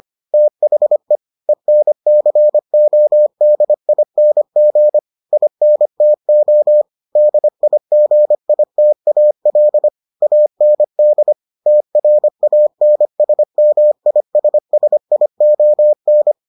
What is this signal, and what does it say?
Morse code, heard as a keyed tone, about 25 wpm, 600 Hz. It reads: THE ENCODING INTO DIGITAL AND TRANSMISSION